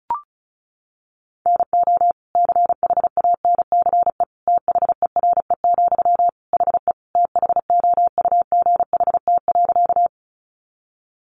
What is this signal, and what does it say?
Morse code, heard as a keyed tone, about 34 wpm, 700 Hz. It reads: NO CHANCE THERE, HE THOUGHT.
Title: Sentences from Top 800 Words 35wpm